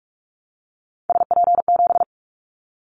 Morse code, heard as a keyed tone, about 44 wpm, 700 Hz. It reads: SP7
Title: Callsign Prefixes for Top 25 DXCC - 45wpm